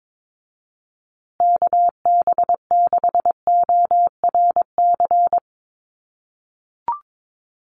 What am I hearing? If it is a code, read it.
Morse code, heard as a keyed tone, about 22 wpm, 700 Hz. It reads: KB6ORC
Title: USA Call Signs Repeated 2x 22wpm